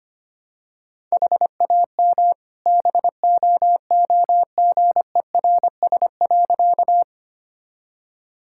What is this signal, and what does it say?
Morse code, heard as a keyed tone, about 25 wpm, 700 Hz. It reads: HAM BOOGERS.
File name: practice / 25wpm / All-Age Appropriate Jokes - 25wpm